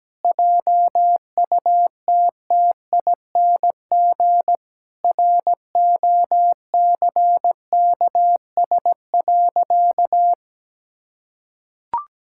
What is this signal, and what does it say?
Morse code, heard as a keyed tone, about 17 wpm, 700 Hz. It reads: JUTTING ROCKS.
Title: Pangram Challenge v2 - 17wpm